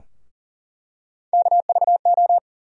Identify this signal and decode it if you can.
Morse code, heard as a keyed tone, about 40 wpm, 700 Hz. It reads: KVO
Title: USA Call Sign Postfixes - 3 Letters 40wpm